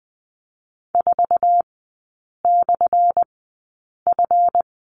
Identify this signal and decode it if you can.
Morse code, heard as a keyed tone, about 20 wpm, spaced as for 8 wpm, 700 Hz. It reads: 4/F
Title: 178 Lesson 29e - Slash - Three Characters 20wpm